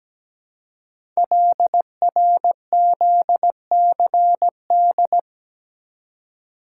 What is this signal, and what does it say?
Morse code, heard as a keyed tone, about 17 wpm, 700 Hz. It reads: LRZCD
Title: Code Groups - 5 Letters - 17wpm